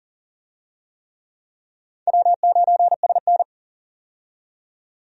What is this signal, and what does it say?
Morse code, heard as a keyed tone, about 40 wpm, 700 Hz. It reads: W9SN
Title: USA Call Signs - 4 Characters - 40wpm